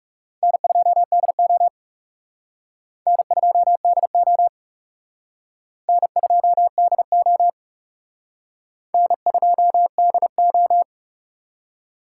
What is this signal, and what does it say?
Morse code, heard as a keyed tone, about 36 wpm, 700 Hz. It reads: N2DO N2DO N2DO N2DO
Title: Speed Racing - USA 1x2 Call Signs 30wpm